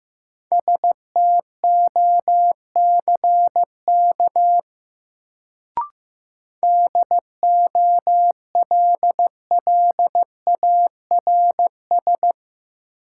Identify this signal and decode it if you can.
Morse code, heard as a keyed tone, about 15 wpm, 700 Hz. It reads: STOCK DOLLARS
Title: Sets of 5 Words - New Word Drill - 15wpm